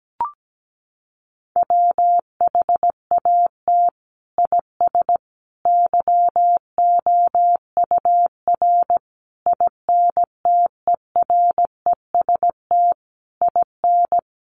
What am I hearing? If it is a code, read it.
Morse code, heard as a keyed tone, about 17 wpm, 700 Hz. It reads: WHAT IS YOUR INTEREST IN
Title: Sentences from Top 1300 Words 17wpm